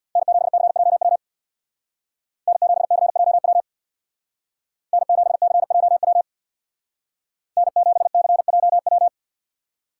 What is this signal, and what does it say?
Morse code, heard as a keyed tone, about 64 wpm, 700 Hz. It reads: N7CJW N7CJW N7CJW N7CJW